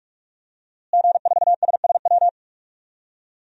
Morse code, heard as a keyed tone, about 43 wpm, 700 Hz. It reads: G4SSW
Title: DXCC-100 Call Signs - 5 Characters - Repeated 3x - 45wpm